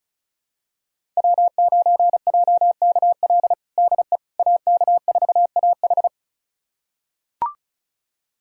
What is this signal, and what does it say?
Morse code, heard as a keyed tone, about 36 wpm, 700 Hz. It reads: W9JKL DE AK4AH
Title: USA Callsign DE Callsign 35wpm